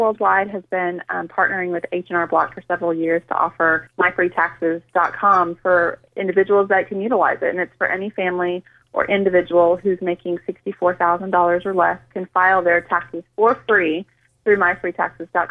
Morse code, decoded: none